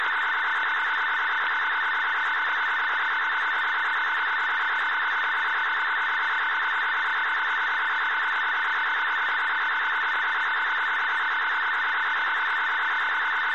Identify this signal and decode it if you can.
Two-tone teleprinter signal, about 170 Hz shift.
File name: uploader / 18-49-40 hz